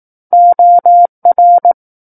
Morse code, heard as a keyed tone, about 18 wpm, 700 Hz. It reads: OR
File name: 18 WPM